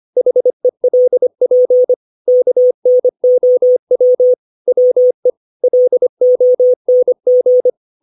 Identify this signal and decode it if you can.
Morse code, heard as a keyed tone, about 25 wpm, 500 Hz. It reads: HELP KNOW WE LONG
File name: 632 words / 632 Words 25WPM